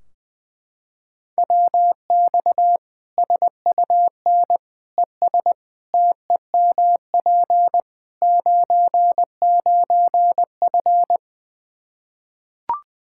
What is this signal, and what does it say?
Morse code, heard as a keyed tone, about 21 wpm, 700 Hz. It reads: WX SUN ES TEMP 99F
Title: Practice QSO - Exchange 2 - Info - 20wpm